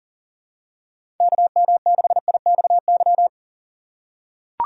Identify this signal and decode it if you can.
Morse code, heard as a keyed tone, about 39 wpm, 700 Hz. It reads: KM6IXY
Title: USA Call Signs - 4x Character Spacing - 40wpm